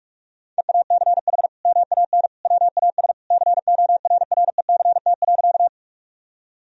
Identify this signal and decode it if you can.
Morse code, heard as a keyed tone, about 47 wpm, 700 Hz. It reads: EACH MAN WAS CORRECT.
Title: Sentences from Top 1000 Words - Review Part 2 - 45wpm